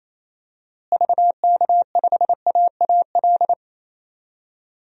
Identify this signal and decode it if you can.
Morse code, heard as a keyed tone, about 28 wpm, 700 Hz. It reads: VK5AAL
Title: DXCC-100 Call Signs - 6 Characters - Repeated 2x - 28wpm